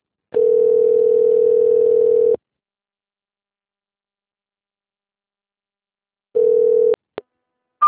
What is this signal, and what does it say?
Ringback tone (North America), 440 + 480 Hz.
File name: rg-1600-unknown- - - .74884